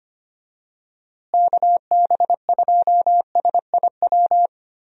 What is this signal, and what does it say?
Morse code, heard as a keyed tone, about 25 wpm, 700 Hz. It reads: KB2SIW
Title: USA Call Signs - 6 Characters - Repeated 3x - 25wpm